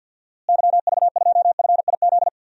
Morse code, heard as a keyed tone, about 50 wpm, 700 Hz. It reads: YV2FIZ
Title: DXCC-100 Call Signs - 6 Characters - Repeated 2x - 50wpm